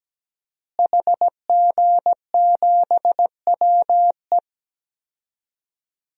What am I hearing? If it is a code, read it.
Morse code, heard as a keyed tone, about 17 wpm, 700 Hz. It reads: HG7WE
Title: DXCC-100 Call Signs - 5 Characters - Repeated 2x - 17wpm